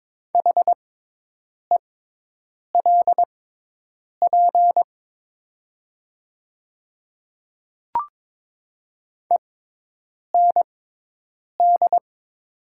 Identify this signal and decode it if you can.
Morse code, heard as a keyed tone, about 22 wpm, spaced as for 8 wpm, 700 Hz. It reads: HELP END
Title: 6x Character Spacing - Top 200 Words - 22wpm